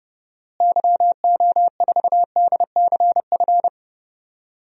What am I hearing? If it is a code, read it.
Morse code, heard as a keyed tone, about 31 wpm, 700 Hz. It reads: YO4DCF
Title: DXCC-100 Call Signs - 6 Characters - Repeated 2x - 30wpm